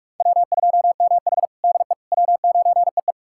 Morse code, heard as a keyed tone, about 46 wpm, 700 Hz. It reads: W2MH DE W9EE
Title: USA Callsign DE Callsign - 4 Characters - 45wpm